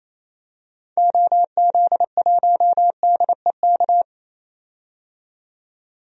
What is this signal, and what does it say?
Morse code, heard as a keyed tone, about 28 wpm, 700 Hz. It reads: OZ1DEK
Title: DXCC-100 Call Signs - 6 Characters - 28wpm